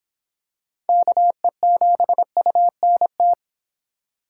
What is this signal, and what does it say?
Morse code, heard as a keyed tone, about 26 wpm, 700 Hz. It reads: KE7UNT